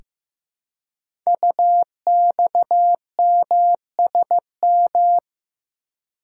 Morse code, heard as a keyed tone, about 15 wpm, 700 Hz. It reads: UXMSM
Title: Code Groups - 5 Letters - 15wpm